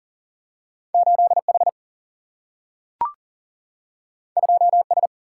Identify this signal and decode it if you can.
Morse code, heard as a keyed tone, about 39 wpm, 700 Hz. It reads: 8H 2S
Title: Single-Letter Single-Number 40wpm